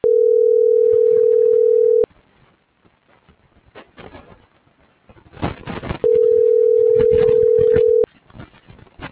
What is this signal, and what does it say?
Ringback tone (North America), 440 + 480 Hz.